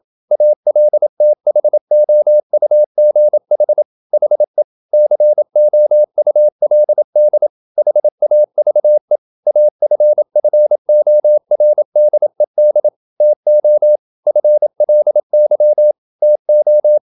Morse code, heard as a keyed tone, about 27 wpm, 600 Hz. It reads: ALTHOUGH HE COULD HAVE AFFORDED TO FLY TO
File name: Never 0027wpm